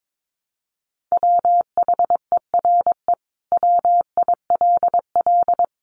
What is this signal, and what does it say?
Morse code, heard as a keyed tone, about 22 wpm, 700 Hz. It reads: WHERE WILL
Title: Sentences from Top 1300 Words 22wpm